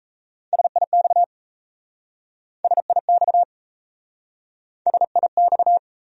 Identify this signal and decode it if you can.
Morse code, heard as a keyed tone, about 38 wpm, 700 Hz. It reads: SIX SIX SIX